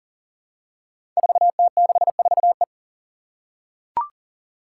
Morse code, heard as a keyed tone, about 40 wpm, 700 Hz. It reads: 4T64E
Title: Five Challenging Letter-Number Combinations 40wpm